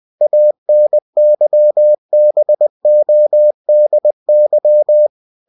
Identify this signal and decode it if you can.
Morse code, heard as a keyed tone, about 20 wpm, 600 Hz. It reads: ANYBODY
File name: cw-books / winnie-the-pooh / winnie-20wpm-page-0106